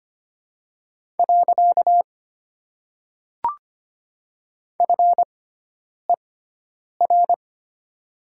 Morse code, heard as a keyed tone, about 25 wpm, 700 Hz. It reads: . F E R
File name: Top 90 QSO Elements - 6x Character Spacing - 25wpm